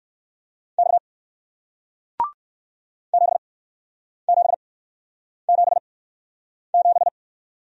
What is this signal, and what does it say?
Morse code, heard as a keyed tone, about 61 wpm, spaced as for 9 wpm, 700 Hz. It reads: X 7777